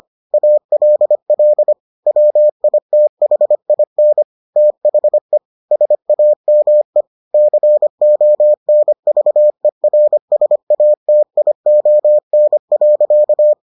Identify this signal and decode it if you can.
Morse code, heard as a keyed tone, about 25 wpm, 600 Hz. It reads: ALL WITHIN THE SAME CONVERSATION.